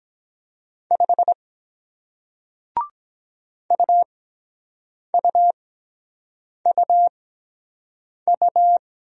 Morse code, heard as a keyed tone, about 23 wpm, spaced as for 7 wpm, 700 Hz. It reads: 5 UUUU